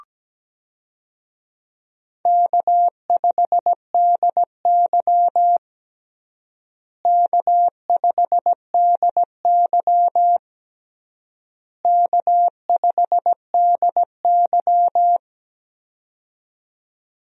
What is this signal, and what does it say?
Morse code, heard as a keyed tone, about 17 wpm, 700 Hz. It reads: K5DY K5DY K5DY
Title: USA Call Signs - 4 Characters - Repeated 3x - 17wpm